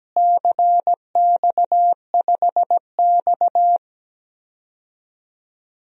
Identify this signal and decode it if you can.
Morse code, heard as a keyed tone, about 17 wpm, 700 Hz. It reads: CX5X
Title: DXCC-100 Call Signs - 4 Characters - Repeated 2x - 17wpm